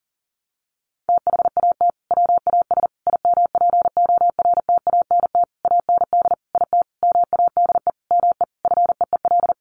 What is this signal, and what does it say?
Morse code, heard as a keyed tone, about 40 wpm, 700 Hz. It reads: THAT WAS IMPORTANT AND IT MADE ME FEEL
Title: Sentences from Top 600 Words 40wpm